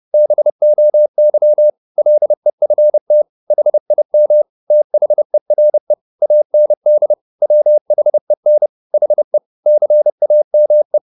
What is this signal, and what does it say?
Morse code, heard as a keyed tone, about 30 wpm, 600 Hz. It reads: BOY LEFT HIM THERE AND WHEN HE CAME
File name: the old man and the sea / the old man and the sea 1 30wpm